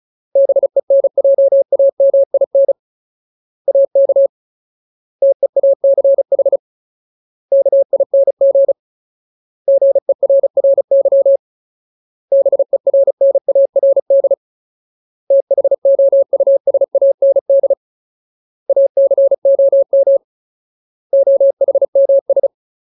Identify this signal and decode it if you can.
Morse code, heard as a keyed tone, about 34 wpm, 550 Hz. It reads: BENJAMIN AK TEACH KING GERRY BERNARD THOUSAND ACOM OHMS
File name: Ultimate Word List / 35WPM Ultimate